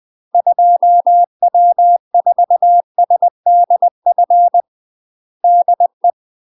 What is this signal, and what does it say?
Morse code, heard as a keyed tone, about 20 wpm, 700 Hz. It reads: 2W4SDF DE
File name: UK MASTER QSOs TRAINING / 20WPM / UK MASTER QSOs 0046